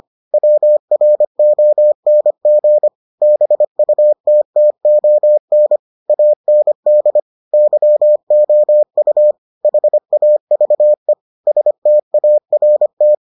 Morse code, heard as a keyed tone, about 25 wpm, 600 Hz. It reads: WRONG BUTTON AND YOU HAVE START